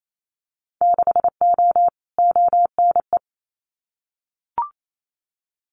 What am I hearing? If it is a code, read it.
Morse code, heard as a keyed tone, about 28 wpm, 700 Hz. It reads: 6O ONE
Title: Winter Field Day Exchanges Encore 28wpm